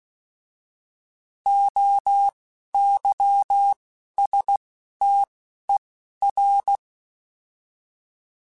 Morse code, heard as a keyed tone, about 16 wpm, spaced as for 11 wpm, 800 Hz. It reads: OYSTER
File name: Alphabetical / Alphabetical mp3s / Os @10